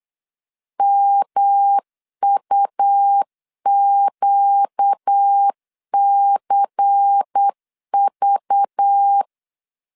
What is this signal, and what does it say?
Morse code, heard as a keyed tone, about 8 wpm, 800 Hz. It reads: MUQCV
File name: LEZIONI / 8 YJ45 / 8 YJ45 8wpm 3214